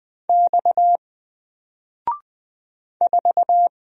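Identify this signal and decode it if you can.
Morse code, heard as a keyed tone, about 20 wpm, 700 Hz. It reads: X 4
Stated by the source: Single letter and number Morse drill at 20 wpm, from N4LCD.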